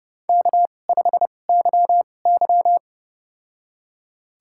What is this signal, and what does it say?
Morse code, heard as a keyed tone, about 30 wpm, spaced as for 22 wpm, 700 Hz. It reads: K5YY
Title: USA Call Signs - 4 Characters - 2x Character Spacing - 30wpm